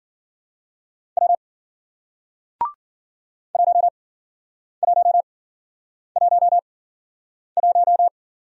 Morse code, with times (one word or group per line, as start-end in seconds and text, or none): 1.2-1.4 W
3.5-8.1 1111